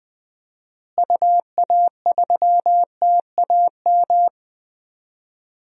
Morse code, heard as a keyed tone, about 21 wpm, 700 Hz. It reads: UA3TAM
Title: DXCC-100 Call Signs - 6 Characters - Repeated 2x - 20wpm